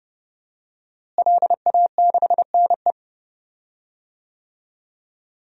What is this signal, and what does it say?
Morse code, heard as a keyed tone, about 30 wpm, 700 Hz. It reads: LA6NE
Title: DXCC-100 Call Signs - 5 Characters - Repeated 3x - 30wpm